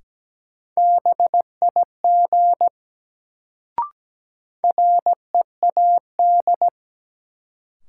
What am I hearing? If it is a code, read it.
Morse code, heard as a keyed tone, about 17 wpm, 700 Hz. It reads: BIG READ
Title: Sentences from Top 200 Words - New Word Drill - 17wpm